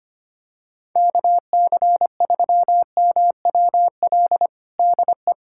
Morse code, heard as a keyed tone, about 25 wpm, 700 Hz. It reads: KC3MWL DE
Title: USA Callsign DE Callsign - 6 Characters - 25wpm